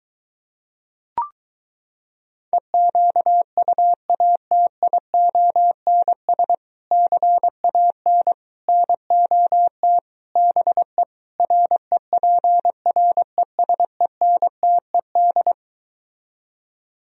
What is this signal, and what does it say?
Morse code, heard as a keyed tone, about 23 wpm, 700 Hz. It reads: EQUATIONS CAN NOT BE REPRESENTED